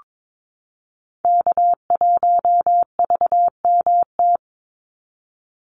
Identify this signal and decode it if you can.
Morse code, heard as a keyed tone, about 22 wpm, 700 Hz. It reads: K1VMT
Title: Call Signs Repeated 2x - 22wpm